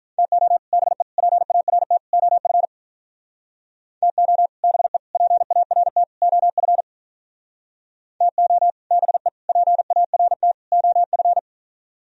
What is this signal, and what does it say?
Morse code, heard as a keyed tone, about 46 wpm, 700 Hz. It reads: TO BE PART OF TO BE PART OF TO BE PART OF